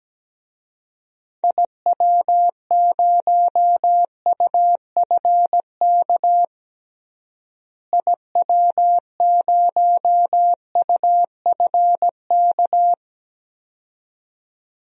Morse code, heard as a keyed tone, about 17 wpm, 700 Hz. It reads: IW0UFK IW0UFK
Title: DXCC-100 Call Signs - 6 Characters - Repeated 2x - 17wpm